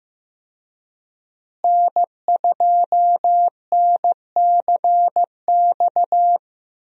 Morse code, heard as a keyed tone, about 15 wpm, 700 Hz. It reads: N2NCX
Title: USA Call Signs - 5 Characters - Repeated 2x - 15wpm